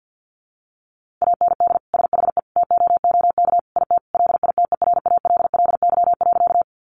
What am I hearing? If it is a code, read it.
Morse code, heard as a keyed tone, about 50 wpm, 700 Hz. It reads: AND SHE TOOK IT LITERALLY.
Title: Sentences from Top 1400 Words 50wpm